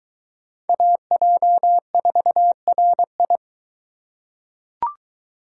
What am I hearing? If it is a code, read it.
Morse code, heard as a keyed tone, about 22 wpm, 700 Hz. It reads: AJ4RI